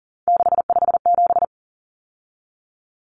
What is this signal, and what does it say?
Morse code, heard as a keyed tone, about 40 wpm, 700 Hz. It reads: 657
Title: Numbers - 3-Digit 40wpm